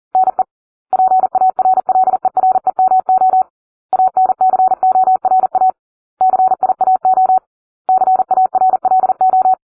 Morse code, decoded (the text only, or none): NE PARLEREMO ANCORA CIAO CARLO